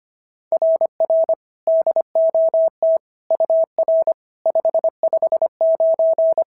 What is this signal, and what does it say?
Morse code, heard as a keyed tone, about 22 wpm, 650 Hz. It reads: RR DOT UR 559
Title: SOTA QSO Practice 25wpm